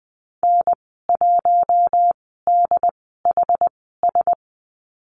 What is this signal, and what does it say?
Morse code, heard as a keyed tone, about 19 wpm, spaced as for 14 wpm, 700 Hz. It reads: N1DHS
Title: 167 Lesson 27g - Number 7 - Call Signs 20wpm